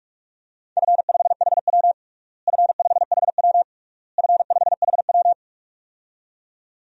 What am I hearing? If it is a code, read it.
Morse code, heard as a keyed tone, about 46 wpm, 700 Hz. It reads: F5HW F5HW F5HW
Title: DXCC-100 Call Signs - 4 Characters - Repeated 3x - 45wpm